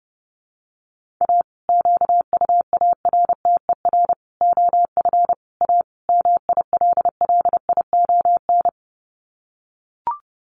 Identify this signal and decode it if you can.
Morse code, heard as a keyed tone, about 30 wpm, 700 Hz. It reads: A QUARTER OF A MILLION
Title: Sets of 5 Words Encore2 30wpm